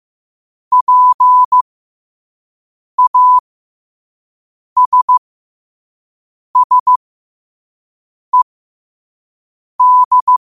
Morse code, heard as a keyed tone, about 15 wpm, spaced as for 5 wpm, 1000 Hz. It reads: PASSED
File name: qotdfiles / 5wpm